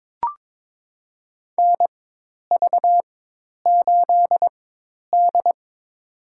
Morse code, heard as a keyed tone, about 22 wpm, spaced as for 10 wpm, 700 Hz. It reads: NV8D
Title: USA Call Signs - 4x Character Spacing - 22wpm